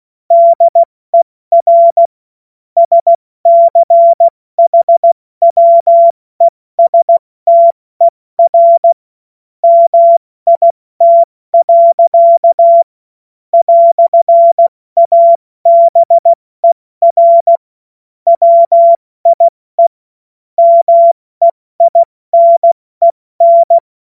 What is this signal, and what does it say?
Morse code, heard as a keyed tone, about 16 wpm, 680 Hz. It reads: DER SCHWESTER MIT. "ABER WIE MEINEN
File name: Morsekurs / Witze / Witz 18. Juli 2023 70 F80